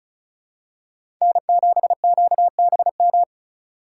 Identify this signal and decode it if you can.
Morse code, heard as a keyed tone, about 36 wpm, 700 Hz. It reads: N7QBM
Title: USA Call Signs - 5 Characters - 4x Character Spacing - 35wpm